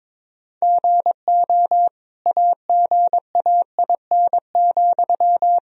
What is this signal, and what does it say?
Morse code, heard as a keyed tone, about 22 wpm, 700 Hz. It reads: GO AGAIN,
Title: Sentences from Top 1000 Words - Review Part 4 - 22wpm